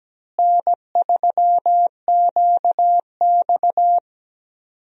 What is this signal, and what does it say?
Morse code, heard as a keyed tone, about 17 wpm, 700 Hz. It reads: N3QX
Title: USA Call Signs - 4 Characters - Repeated 3x - 17wpm